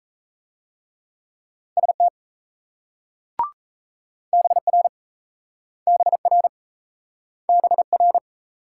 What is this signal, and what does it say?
Morse code, heard as a keyed tone, about 37 wpm, 700 Hz. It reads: ST BR BR BR